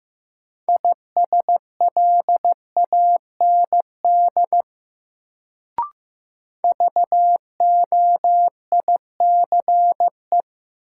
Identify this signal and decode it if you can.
Morse code, heard as a keyed tone, about 15 wpm, 700 Hz. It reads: ISLAND VOICE
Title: Sentences from Top 500 Words - New Word Drill - 15wpm